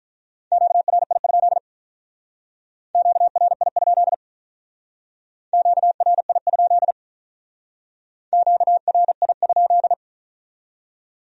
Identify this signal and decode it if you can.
Morse code, heard as a keyed tone, about 44 wpm, 700 Hz. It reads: QRI? QRI? QRI? QRI?